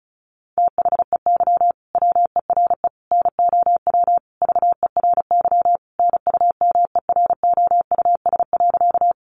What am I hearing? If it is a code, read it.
Morse code, heard as a keyed tone, about 34 wpm, 700 Hz. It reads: THEY WERE NOW VERY NUMEROUS.
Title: Sentences from Top 1400 Words 35wpm